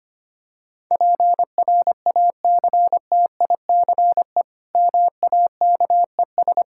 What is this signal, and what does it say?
Morse code, heard as a keyed tone, about 25 wpm, 700 Hz. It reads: PRACTICE MAKES
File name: practice / 25wpm / Difficult Sentences with Hints - 25wpm